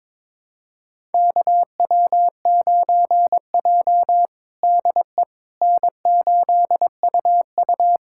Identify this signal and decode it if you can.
Morse code, heard as a keyed tone, about 21 wpm, 700 Hz. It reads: KW9J DE N8UU
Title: USA Callsign DE Callsign - 4 Characters - 22wpm